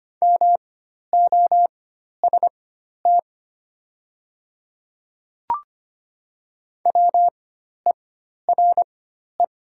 Morse code, heard as a keyed tone, about 25 wpm, spaced as for 12 wpm, 700 Hz. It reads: MOST WERE
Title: 4x Character Spacing - Top 100 Words - 25wpm